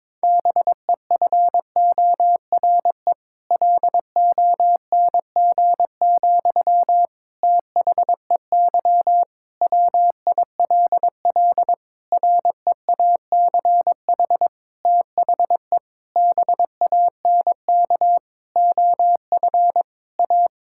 Morse code, heard as a keyed tone, about 22 wpm, 700 Hz. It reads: BEFORE LONG, THEY WILL REACH THE BANK OF A
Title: Sentences from Top 1000 Words - Review Part 6 - 22wpm